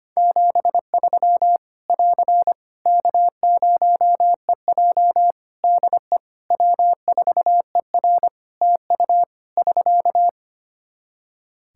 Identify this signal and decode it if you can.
Morse code, heard as a keyed tone, about 25 wpm, 700 Hz. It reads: 73 <AR> K0EJ DE W4ER TU <SK>
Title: Practice QSO - Exchange 4 - Ending - 25wpm